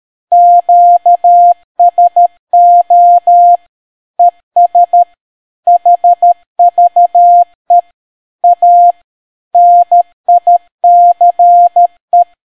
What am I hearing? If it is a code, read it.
Morse code, heard as a keyed tone, about 13 wpm, 700 Hz. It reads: QSO ES HVE A NICE